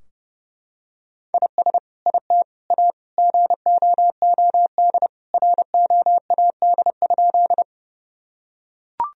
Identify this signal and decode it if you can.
Morse code, heard as a keyed tone, about 30 wpm, 700 Hz. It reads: IS IT A GOOD ROAD?
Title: Sentences from Top 1000 Words - Review Part 2 - 30wpm